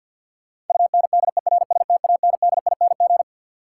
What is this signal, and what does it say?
Morse code, heard as a keyed tone, about 53 wpm, 700 Hz. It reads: UNDERSTANDING
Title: Sentences from Top 1000 Words - Review Part 3 - New Word Drill - 50wpm